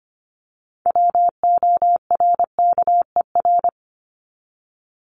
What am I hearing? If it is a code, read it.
Morse code, heard as a keyed tone, about 25 wpm, 700 Hz. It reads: WORKER
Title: Top 1751-2000 Words 25wpm